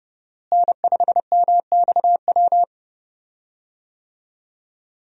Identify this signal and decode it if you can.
Morse code, heard as a keyed tone, about 30 wpm, 700 Hz. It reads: N5MXW
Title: USA Call Signs - 5 Characters - Repeated 3x - 30wpm